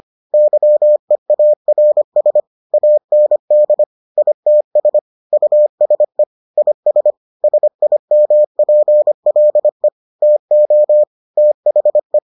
Morse code, heard as a keyed tone, about 25 wpm, 600 Hz. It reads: YEARS AND ITS USE IS SIMPLE TO THE